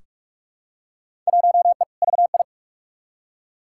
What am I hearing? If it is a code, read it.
Morse code, heard as a keyed tone, about 44 wpm, 700 Hz. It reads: 1E VI
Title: Field Day Exchanges Encore - 45wpm